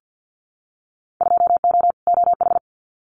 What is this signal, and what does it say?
Morse code, heard as a keyed tone, about 50 wpm, 700 Hz. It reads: 2O OH
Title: Winter Field Day Exchanges 50wpm